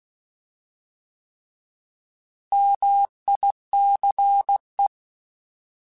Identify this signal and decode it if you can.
Morse code, heard as a keyed tone, about 16 wpm, 800 Hz. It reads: MICE
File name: Alphabetical / Alphabetical mp3s / Ms @15